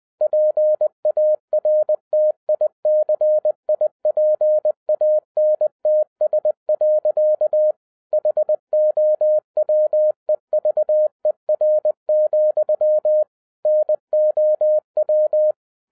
Morse code, heard as a keyed tone, about 20 wpm, 610 Hz. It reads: PARTICIPANTS. HOWEVER, NOW
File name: 2012-01-mp3 / n16-THROUGH EIGHT MONTHS-20wpm